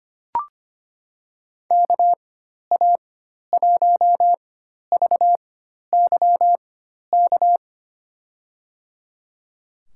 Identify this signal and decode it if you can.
Morse code, heard as a keyed tone, about 25 wpm, spaced as for 12 wpm, 700 Hz. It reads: KA1VYK
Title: USA Call Signs - 6 Characters - 4x Character Spacing - 25wpm